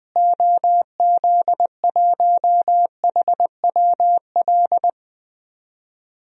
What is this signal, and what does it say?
Morse code, heard as a keyed tone, about 20 wpm, 700 Hz. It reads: OZ1HWL
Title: DXCC-100 Call Signs - 6 Characters - Repeated 3x - 20wpm